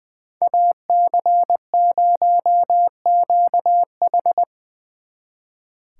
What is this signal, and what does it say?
Morse code, heard as a keyed tone, about 25 wpm, 700 Hz. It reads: AC0QH
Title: Speed Racing - USA 2x2 Call Signs 20wpm